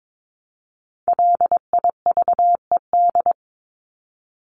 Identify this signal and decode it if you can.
Morse code, heard as a keyed tone, about 22 wpm, 700 Hz. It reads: LIVED
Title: ED Suffix - Top 100 Words with Base Words 22wpm